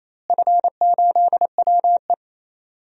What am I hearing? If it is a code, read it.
Morse code, heard as a keyed tone, about 28 wpm, 700 Hz. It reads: F8WE
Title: DXCC-100 Call Signs - 4 Characters - Repeated 2x - 28wpm